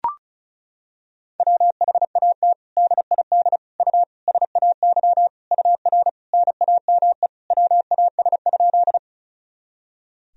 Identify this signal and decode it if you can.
Morse code, heard as a keyed tone, about 34 wpm, 700 Hz. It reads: WHAT DID U SAY UR NAME WAS?
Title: Sentences from Top 200 Words and Abbreviations 35wpm